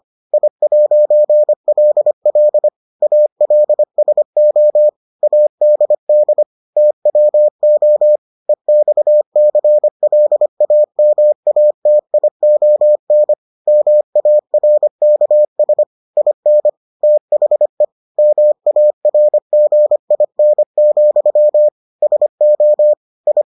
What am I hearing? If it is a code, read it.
Morse code, heard as a keyed tone, about 25 wpm, 600 Hz. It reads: I'LL ALSO ADD TWO EXCLAMATION MARKS IN THE MARGIN, SO I